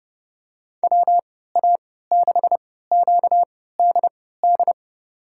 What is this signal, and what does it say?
Morse code, heard as a keyed tone, about 30 wpm, spaced as for 17 wpm, 700 Hz. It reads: WA6QDD
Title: USA Call Signs - 6 Characters - 3x Character Spacing - 30wpm